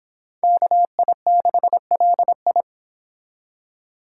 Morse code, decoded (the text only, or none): KI6LI